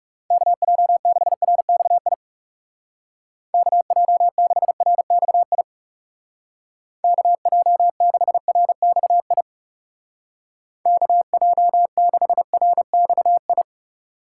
Morse code, heard as a keyed tone, about 36 wpm, 700 Hz. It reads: KJ6RXI KJ6RXI KJ6RXI KJ6RXI
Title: Speed Racing - USA 2x3 Call Signs 30wpm